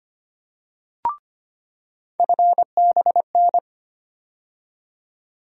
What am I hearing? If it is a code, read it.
Morse code, heard as a keyed tone, about 25 wpm, 700 Hz. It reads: FBN
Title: USA Call Sign Postfixes - 3 Letters 25wpm